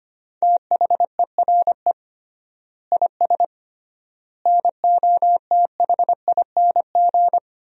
Morse code, heard as a keyed tone, about 25 wpm, 700 Hz. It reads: THERE IS NOTHING
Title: 3x Word-Spacing - Sentences from Top 500 Words - 25wpm